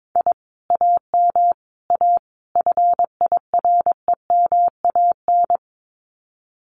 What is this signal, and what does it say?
Morse code, heard as a keyed tone, about 22 wpm, 700 Hz. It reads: I AM A FIREMAN
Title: Common QSO Phrases Encore 22wpm